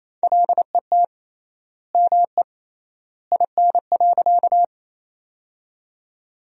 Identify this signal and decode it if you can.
Morse code, heard as a keyed tone, about 28 wpm, 700 Hz. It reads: LET ME IN.
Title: 3x Word-Spacing - Sentences from Top 300 Words - 28wpm